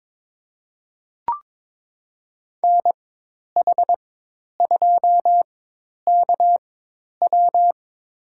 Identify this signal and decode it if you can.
Morse code, heard as a keyed tone, about 22 wpm, spaced as for 10 wpm, 700 Hz. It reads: NH2KW
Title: USA Call Signs - 4x Character Spacing - 22wpm